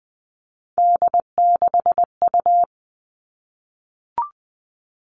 Morse code, heard as a keyed tone, about 20 wpm, 700 Hz. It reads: D6U
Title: All Cut Numbers - 3-Digit 20wpm